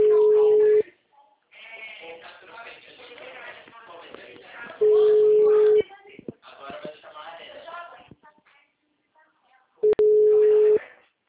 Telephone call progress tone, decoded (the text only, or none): ringback tone